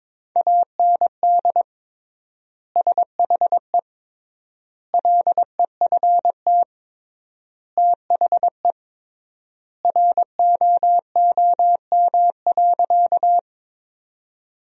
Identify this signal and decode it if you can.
Morse code, heard as a keyed tone, about 22 wpm, 700 Hz. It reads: AND SHE LEFT THE ROOM.
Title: 3x Word-Spacing - Sentences from Top 300 Words - 22wpm